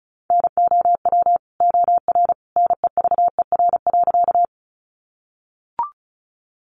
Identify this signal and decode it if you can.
Morse code, heard as a keyed tone, about 34 wpm, 700 Hz. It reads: NOW OR NEVER.
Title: Sentences from Top 300 Words 35wpm